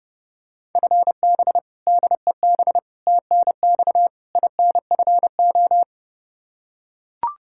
Morse code, heard as a keyed tone, about 32 wpm, 700 Hz. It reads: FB DEB TNX INFO
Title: K1USN SST QSO Practice 30wpm